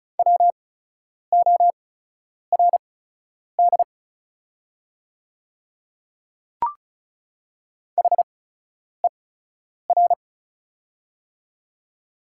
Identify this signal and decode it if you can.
Morse code, heard as a keyed tone, about 36 wpm, spaced as for 10 wpm, 690 Hz. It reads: WORD HER
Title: 8x Character Spacing - Top 100 Words - 35wpm